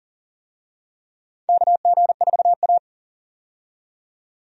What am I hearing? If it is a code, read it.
Morse code, heard as a keyed tone, about 37 wpm, 700 Hz. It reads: KG4A
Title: USA Call Signs - 4 Characters - 40wpm